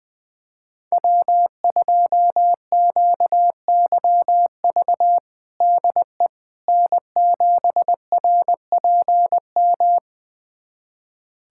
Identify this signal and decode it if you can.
Morse code, heard as a keyed tone, about 21 wpm, 700 Hz. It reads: W2QYV DE N7RPM
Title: USA Callsign DE Callsign - 5 Characters - 20wpm